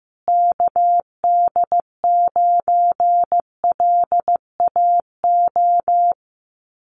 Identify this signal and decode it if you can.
Morse code, heard as a keyed tone, about 15 wpm, 700 Hz. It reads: KD9LAO
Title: Common QSO Phrases Encore 15wpm